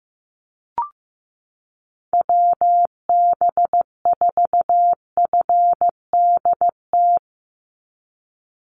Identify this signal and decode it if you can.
Morse code, heard as a keyed tone, about 15 wpm, 700 Hz. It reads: WB4FDT
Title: Call Signs Repeated 2x - 15wpm